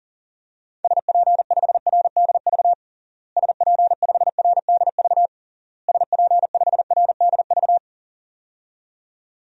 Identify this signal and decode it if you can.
Morse code, heard as a keyed tone, about 40 wpm, 700 Hz. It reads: SP5RDV SP5RDV SP5RDV
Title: DXCC-100 Call Signs - 6 Characters - Repeated 3x - 40wpm